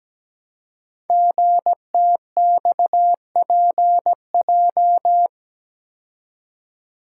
Morse code, heard as a keyed tone, about 17 wpm, 700 Hz. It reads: GTXPJ
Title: Code Groups - 5 Letters - 17wpm